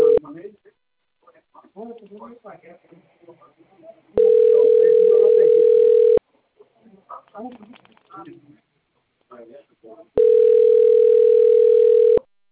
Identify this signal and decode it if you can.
Ringback tone (North America), 440 + 480 Hz, with speech.
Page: Americana 1